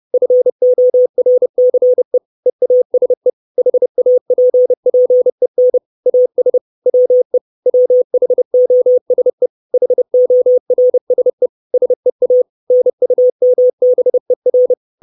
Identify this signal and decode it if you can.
Morse code, heard as a keyed tone, about 30 wpm, 500 Hz. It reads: FORCE EASE HAPPEN AS WE WHOSE HORSE SEA NUMBER
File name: 632 Words 30WPM